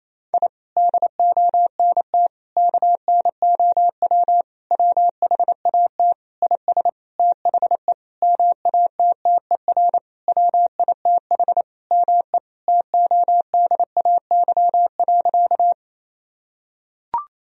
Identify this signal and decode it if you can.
Morse code, heard as a keyed tone, about 28 wpm, 700 Hz. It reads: I DONT KNOW WHAT IS THE MATTER WITH ME TODAY.
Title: Sentences from Top 1000 Words - Review Part 4 - 28wpm